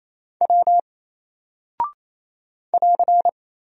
Morse code, heard as a keyed tone, about 27 wpm, 700 Hz. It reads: W <AR>
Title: Single Character (Letter:Number:Punctuation:Prosign) 28wpm